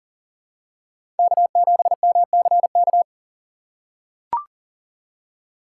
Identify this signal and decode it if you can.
Morse code, heard as a keyed tone, about 40 wpm, 700 Hz. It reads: K7MCK
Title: USA Call Signs - 5 Characters - 6x Character Spacing - 40wpm